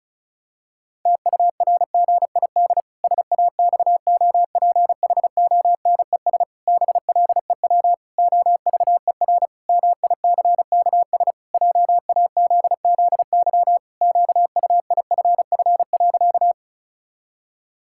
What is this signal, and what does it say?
Morse code, heard as a keyed tone, about 36 wpm, 700 Hz. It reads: TURGID SAXOPHONES BLEW OVER MICKS JAZZY QUIFF.
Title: Pangram Challenge v2 - 35wpm